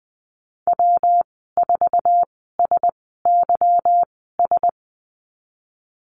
Morse code, heard as a keyed tone, about 20 wpm, spaced as for 14 wpm, 700 Hz. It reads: W4SYS
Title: 154 Lesson 25g - Letter p - Call Signs 20wpm